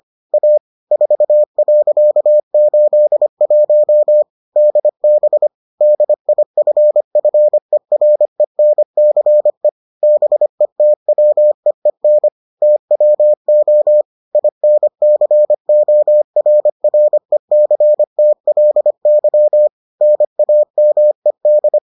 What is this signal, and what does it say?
Morse code, heard as a keyed tone, about 25 wpm, 600 Hz. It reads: A 4.81 DB DIFFERENCE BETWEEN TWO INCORRECTLY NAMED